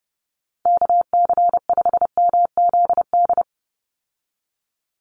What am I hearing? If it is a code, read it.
Morse code, heard as a keyed tone, about 30 wpm, 700 Hz. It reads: KC5MZD
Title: USA Call Signs 30wpm